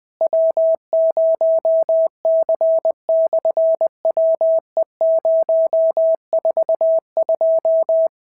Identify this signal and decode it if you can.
Morse code, heard as a keyed tone, about 20 wpm, 650 Hz. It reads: W0C/WE042
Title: SOTA QSO Practice 20wpm